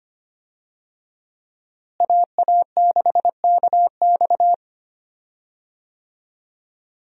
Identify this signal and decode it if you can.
Morse code, heard as a keyed tone, about 25 wpm, 700 Hz. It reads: AA6KX
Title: USA Call Signs - 5 Characters - Repeated 2x - 25wpm